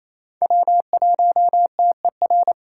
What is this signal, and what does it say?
Morse code, heard as a keyed tone, about 27 wpm, 700 Hz. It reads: W1TER
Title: K1USN SST QSO Practice - 28wpm